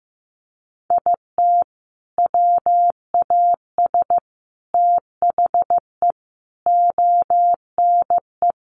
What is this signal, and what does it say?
Morse code, heard as a keyed tone, about 15 wpm, 700 Hz. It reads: IT WAS THE ONE
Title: Sentences from Top 1400 Words 15wpm